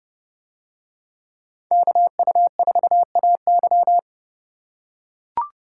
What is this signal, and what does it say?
Morse code, heard as a keyed tone, about 28 wpm, 700 Hz. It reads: KU4AY